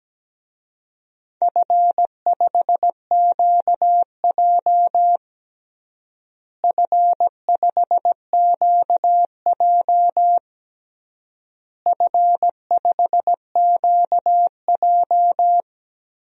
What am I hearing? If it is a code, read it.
Morse code, heard as a keyed tone, about 17 wpm, 700 Hz. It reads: F5QJ F5QJ F5QJ
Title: DXCC-100 Call Signs - 4 Characters - Repeated 3x - 17wpm